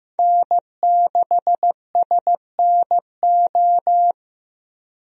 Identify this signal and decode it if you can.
Morse code, heard as a keyed tone, about 16 wpm, 700 Hz. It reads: N6SNO
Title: USA Call Signs - 5 Characters - Repeated 2x - 15wpm